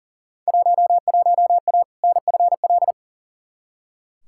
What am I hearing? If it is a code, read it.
Morse code, heard as a keyed tone, about 40 wpm, 700 Hz. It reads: 11A NFL
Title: Field Day Exchanges Encore - 40wpm